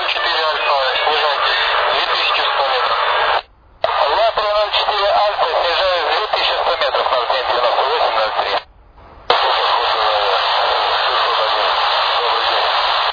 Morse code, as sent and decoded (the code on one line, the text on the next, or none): none